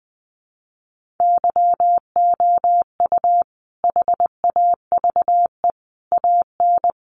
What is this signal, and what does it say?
Morse code, heard as a keyed tone, about 19 wpm, 700 Hz. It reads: YOU HAVE AN
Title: Sentences from Top 1300 Words 20wpm